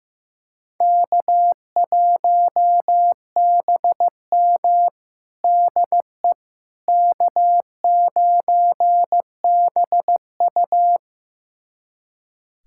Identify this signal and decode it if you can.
Morse code, heard as a keyed tone, about 16 wpm, 700 Hz. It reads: K1BM DE K9BU
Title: USA Callsign DE Callsign - 4 Characters - 15wpm